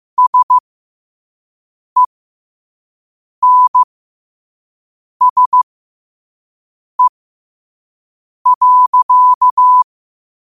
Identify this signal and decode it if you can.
Morse code, heard as a keyed tone, about 15 wpm, spaced as for 5 wpm, 1000 Hz. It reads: SENSE.
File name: qotdfiles / 5wpm